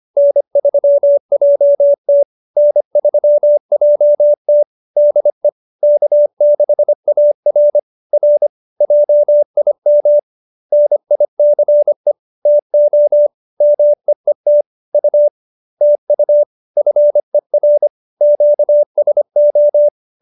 Morse code, heard as a keyed tone, about 25 wpm, 590 Hz. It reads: N3JT N3JT DE K6AR R JIM NICE TO MEET U TU FER QSO